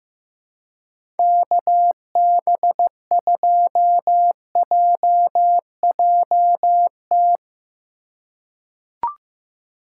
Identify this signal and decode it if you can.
Morse code, heard as a keyed tone, about 15 wpm, 700 Hz. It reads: KB2JJT
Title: USA Call Signs - 6 Characters - 3x Character Spacing - 15wpm